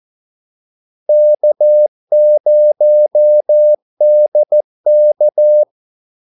Morse code, heard as a keyed tone, about 14 wpm, 600 Hz. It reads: K0DK
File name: Callsigns-14wpm